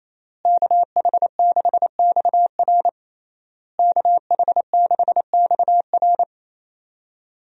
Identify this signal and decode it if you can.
Morse code, heard as a keyed tone, about 28 wpm, 700 Hz. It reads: KH6XR KH6XR
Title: DXCC-100 Call Signs - 5 Characters - Repeated 3x - 28wpm